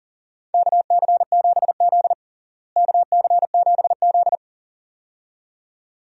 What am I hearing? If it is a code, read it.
Morse code, heard as a keyed tone, about 40 wpm, 700 Hz. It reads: KC7Z KC7Z
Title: USA Call Signs - 4 Characters - Repeated 2x - 40wpm